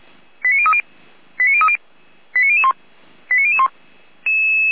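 Five-tone selective call (ZVEI), heard as quick runs of five tones, tones of about 70 ms. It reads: 89030 89030 89001 89001
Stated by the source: ZVEI five-tone selective call received by SDR.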